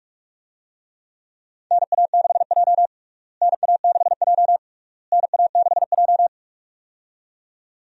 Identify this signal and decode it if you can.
Morse code, heard as a keyed tone, about 45 wpm, 700 Hz. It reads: NA6J NA6J NA6J
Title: USA Call Signs - 4 Characters - Repeated 3x - 45wpm